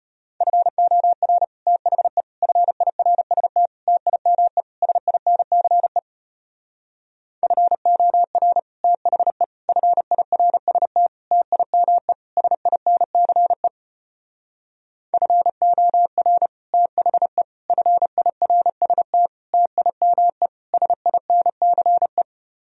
Morse code, heard as a keyed tone, about 33 wpm, 700 Hz. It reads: FOR THE FIRST TIME SINCE FOR THE FIRST TIME SINCE FOR THE FIRST TIME SINCE